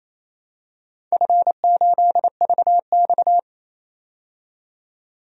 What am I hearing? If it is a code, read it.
Morse code, heard as a keyed tone, about 28 wpm, 700 Hz. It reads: F8VX
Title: DXCC-100 Call Signs - 4 Characters - 28wpm